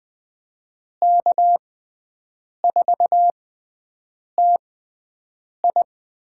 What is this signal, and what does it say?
Morse code, heard as a keyed tone, about 20 wpm, spaced as for 7 wpm, 700 Hz. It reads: K4TI
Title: USA Call Signs - 6 Characters - 6x Character Spacing - 20wpm